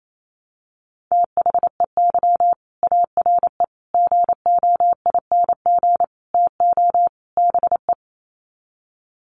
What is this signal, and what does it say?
Morse code, heard as a keyed tone, about 28 wpm, 700 Hz. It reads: THEY ARE GOING TO BE
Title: Sets of 5 Words Encore2 28wpm